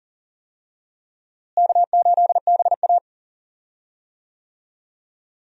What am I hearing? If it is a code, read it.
Morse code, heard as a keyed tone, about 40 wpm, 700 Hz. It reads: K8BA
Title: USA Call Signs - 4 Characters - Repeated 3x - 40wpm